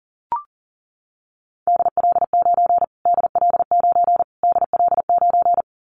Morse code, heard as a keyed tone, about 40 wpm, 700 Hz. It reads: DL9 DL9 DL9
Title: Callsign Prefixes for Top 25 DXCC - Repeated 3x - 40wpm